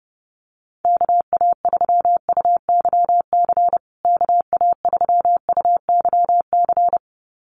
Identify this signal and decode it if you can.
Morse code, heard as a keyed tone, about 30 wpm, 700 Hz. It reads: KA3UYC KA3UYC
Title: USA Call Signs Repeated 2x 30wpm